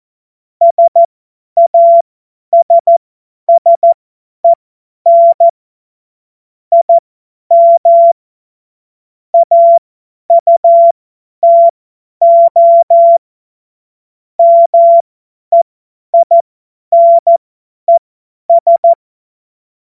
Morse code, heard as a keyed tone, about 14 wpm, spaced as for 10 wpm, 680 Hz. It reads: SASSEN IM AUTO MEINES
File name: Morsekurs / Witze / Witz 26. Oktober 2021, 50 F70